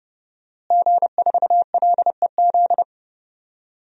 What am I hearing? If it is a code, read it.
Morse code, heard as a keyed tone, about 30 wpm, 700 Hz. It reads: G4LEZ
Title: DXCC-100 Call Signs - 5 Characters - Repeated 2x - 30wpm